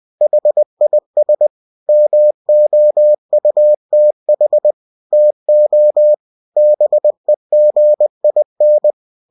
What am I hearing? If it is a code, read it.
Morse code, heard as a keyed tone, about 20 wpm, 600 Hz. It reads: HIS MOUTH TO BEGIN